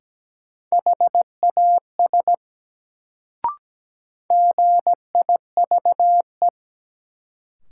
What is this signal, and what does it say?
Morse code, heard as a keyed tone, about 17 wpm, 700 Hz. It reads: HAS GIVE
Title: Sets of 2 Words - New Word Drill - 17wpm